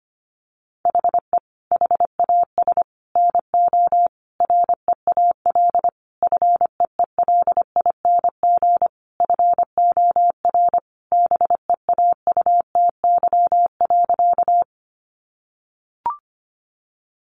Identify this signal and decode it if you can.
Morse code, heard as a keyed tone, about 25 wpm, 700 Hz. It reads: HE HAS NO REAL FEELING FOR BEAUTY.
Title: Sentences from Top 1200 Words 25wpm